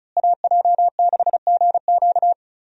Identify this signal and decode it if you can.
Morse code, heard as a keyed tone, about 34 wpm, 700 Hz. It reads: AJ6GQ
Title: USA Call Signs - 5 Characters - Repeated 3x - 35wpm